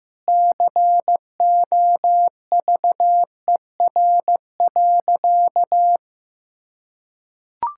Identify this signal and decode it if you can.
Morse code, heard as a keyed tone, about 15 wpm, 700 Hz. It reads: COVER.
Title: Sentences from Top 1000 Words - Review Part 6 - 15wpm